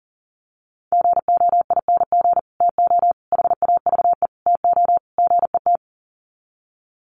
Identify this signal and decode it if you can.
Morse code, heard as a keyed tone, about 40 wpm, 700 Hz. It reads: GOING TO HAVE TO GET
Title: Sets of 5 Words Encore2 40wpm